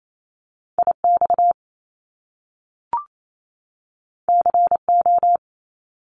Morse code, heard as a keyed tone, about 28 wpm, 700 Hz. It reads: IX CO
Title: Top Two Letter Combinations in Words 28wpm